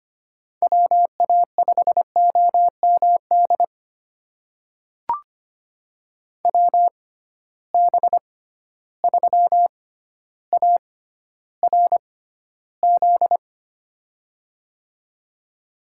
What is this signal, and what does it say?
Morse code, heard as a keyed tone, about 25 wpm, spaced as for 13 wpm, 700 Hz. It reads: WA5OMD WB3ARZ
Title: USA Call Signs - 6 Characters - 6x Character Spacing - 25wpm